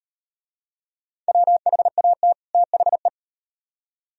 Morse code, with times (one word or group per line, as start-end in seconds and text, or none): 1.3-2.3 WHAT
2.5-3.1 THE